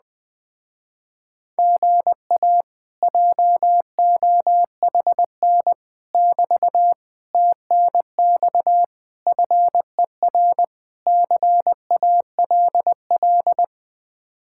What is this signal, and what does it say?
Morse code, heard as a keyed tone, about 20 wpm, 700 Hz. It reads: GA JOHN <BT> TNX FER CALL
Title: Speed-Racing Common QSO Phrases Encore - 17wpm